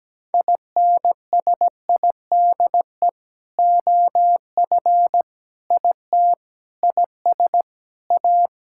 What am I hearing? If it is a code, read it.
Morse code, heard as a keyed tone, about 17 wpm, 700 Hz. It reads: INSIDE OF IT IS A
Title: Sentences from Top 1000 Words - Review Part 6 - 17wpm